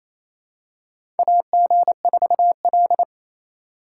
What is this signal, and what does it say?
Morse code, heard as a keyed tone, about 29 wpm, 700 Hz. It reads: AG4L
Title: USA Call Signs - 4 Characters - 4x Character Spacing - 28wpm